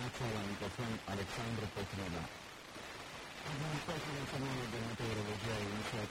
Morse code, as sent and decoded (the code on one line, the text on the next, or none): none